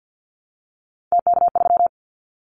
Morse code, heard as a keyed tone, about 50 wpm, 700 Hz. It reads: TK3
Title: Callsign Prefixes for Top 100 DXCC - 50wpm